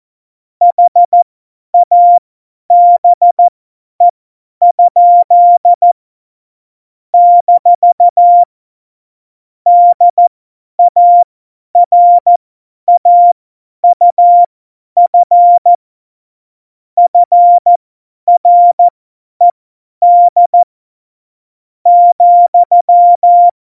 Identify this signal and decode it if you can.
Morse code, heard as a keyed tone, about 14 wpm, spaced as for 9 wpm, 700 Hz. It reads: HABE? - DARAUF FRED,
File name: Morsekurs / Witze / Witz 05. Dezember 2017, 50 F70